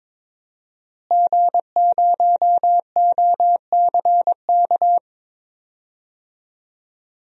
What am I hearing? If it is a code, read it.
Morse code, heard as a keyed tone, about 22 wpm, 700 Hz. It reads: G0OCK
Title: DXCC-100 Call Signs - 5 Characters - Repeated 2x - 22wpm